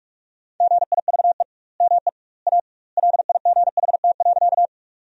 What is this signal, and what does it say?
Morse code, heard as a keyed tone, about 46 wpm, 700 Hz. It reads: GIVE ME A LIGHT.
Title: 2x Word-Spacing - Sentences from Top 200 Words - 45wpm